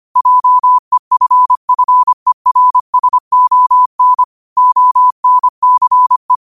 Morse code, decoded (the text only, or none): JEFFERSON ONCE